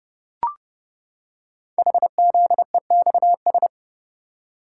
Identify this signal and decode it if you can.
Morse code, heard as a keyed tone, about 30 wpm, 700 Hz. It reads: HZEXS
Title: Code Groups - 5 Letters - 30wpm